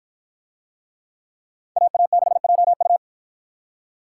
Morse code, heard as a keyed tone, about 53 wpm, 700 Hz. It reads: AA6JU